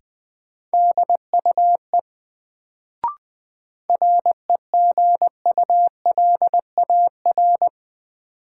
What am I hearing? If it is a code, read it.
Morse code, heard as a keyed tone, about 20 wpm, 700 Hz. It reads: DUE REGULAR
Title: Sentences from Top 900 Words - New Word Drill - 20wpm